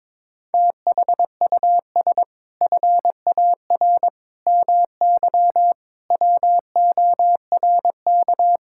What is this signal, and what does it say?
Morse code, heard as a keyed tone, about 22 wpm, 700 Hz. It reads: THUS FAR MY WORK
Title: Sentences from Top 1000 Words - Review Part 5 - 22wpm